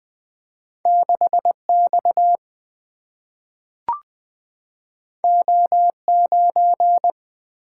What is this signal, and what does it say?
Morse code, heard as a keyed tone, about 20 wpm, 700 Hz. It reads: 6X O9
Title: Single-Letter Single-Number 20wpm